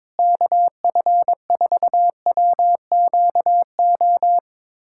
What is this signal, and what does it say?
Morse code, heard as a keyed tone, about 22 wpm, 700 Hz. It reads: KF4WQO
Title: USA Callsign DE Callsign - 6 Characters - 22wpm